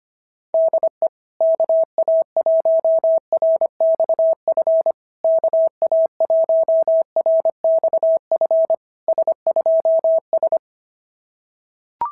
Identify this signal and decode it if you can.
Morse code, heard as a keyed tone, about 25 wpm, 650 Hz. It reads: DE KA1RXF KA1RXF S2S
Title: SOTA QSO Practice 25wpm